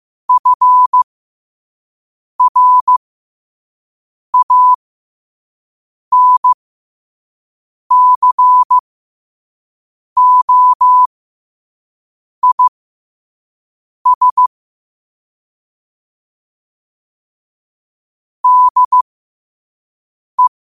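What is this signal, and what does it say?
Morse code, heard as a keyed tone, about 15 wpm, spaced as for 5 wpm, 1000 Hz. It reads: FRANCOIS DE
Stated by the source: Morse code practice at 5 wpm.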